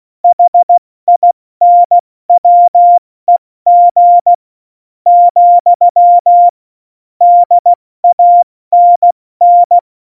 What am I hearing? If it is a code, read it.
Morse code, heard as a keyed tone, about 16 wpm, spaced as for 13 wpm, 700 Hz. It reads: HINWEG, DANN
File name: Morsekurs / Witze / MP3, Witz 08. Mai 2018, 70 F80